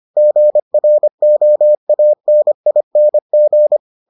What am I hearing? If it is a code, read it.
Morse code, heard as a keyed tone, about 25 wpm, 600 Hz. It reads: GROANING